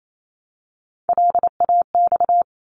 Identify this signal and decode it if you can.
Morse code, heard as a keyed tone, about 28 wpm, 700 Hz. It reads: LAX
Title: ARRL Sections 28wpm